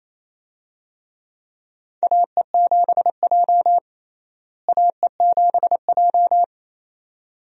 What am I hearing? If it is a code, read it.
Morse code, heard as a keyed tone, about 28 wpm, 700 Hz. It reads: AE7J AE7J
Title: USA Call Signs - 4 Characters - Repeated 2x - 28wpm